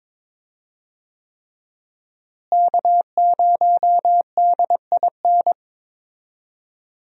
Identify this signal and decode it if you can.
Morse code, heard as a keyed tone, about 22 wpm, 700 Hz. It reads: K0DIN
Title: USA Call Signs - 5 Characters - Repeated 3x - 22wpm